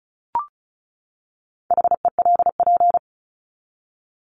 Morse code, heard as a keyed tone, about 34 wpm, 700 Hz. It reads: HELP
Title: Top 200 Words 35wpm